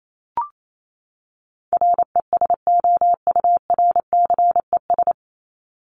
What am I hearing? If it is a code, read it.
Morse code, heard as a keyed tone, about 28 wpm, 700 Hz. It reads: RESOURCES
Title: ES:S Suffix - Top 100 Words with Base Words 28wpm